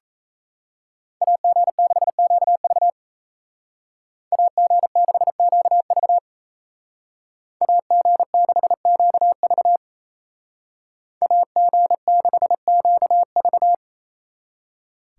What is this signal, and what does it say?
Morse code, heard as a keyed tone, about 36 wpm, 700 Hz. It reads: AG6QV AG6QV AG6QV AG6QV